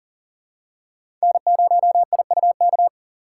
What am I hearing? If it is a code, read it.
Morse code, heard as a keyed tone, about 40 wpm, 700 Hz. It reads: N0IUK
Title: USA Call Signs - 5 Characters - 2x Character Spacing - 40wpm